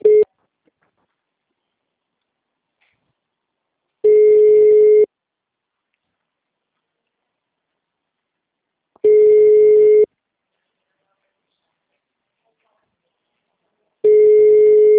Ringback tone (Europe, 425 Hz).